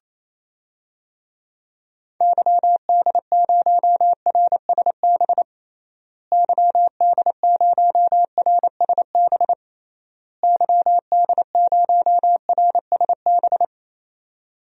Morse code, heard as a keyed tone, about 28 wpm, 700 Hz. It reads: YD0RSB YD0RSB YD0RSB
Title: DXCC-100 Call Signs - 6 Characters - Repeated 3x - 28wpm